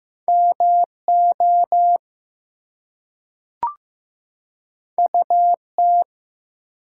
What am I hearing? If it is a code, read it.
Morse code, heard as a keyed tone, about 15 wpm, 700 Hz. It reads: MO UT
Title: US States and Canadian Provinces 15wpm